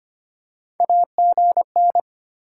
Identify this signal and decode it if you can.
Morse code, heard as a keyed tone, about 25 wpm, 700 Hz. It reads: AGN
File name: Top 20 QSO Elements - 3x Character Spacing - 25wpm